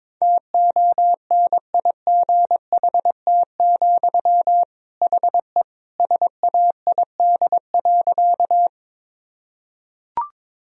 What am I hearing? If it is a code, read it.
Morse code, heard as a keyed tone, about 21 wpm, 700 Hz. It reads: TONIGHT, HE SAID.
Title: Sentences from Top 1000 Words - Review Part 3 - 22wpm